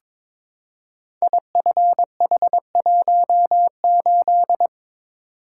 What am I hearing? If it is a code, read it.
Morse code, heard as a keyed tone, about 23 wpm, 690 Hz. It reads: IFH18
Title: Code Groups - 5 Letter:Numbers - 22wpm